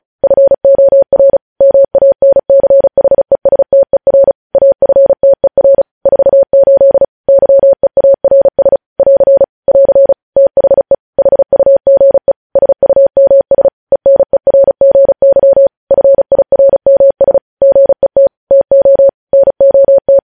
Morse code, heard as a keyed tone, about 36 wpm, 550 Hz. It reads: FOR MANCHESTER AFTER 48 YEARS <AR> <AR> THE HUGE SUMS ENERGY FIRMS GET TO NOT
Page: news headlines in morse code at 35 wpm